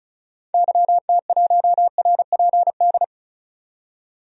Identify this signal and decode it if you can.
Morse code, heard as a keyed tone, about 35 wpm, 700 Hz. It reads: YT1RPD
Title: DXCC-100 Call Signs - 6 Characters - 35wpm